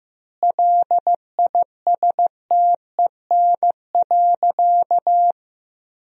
Morse code, heard as a keyed tone, about 15 wpm, 700 Hz. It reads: LISTEN.
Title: 4x Word-Spacing - Sentences from Top 500 Words - 15wpm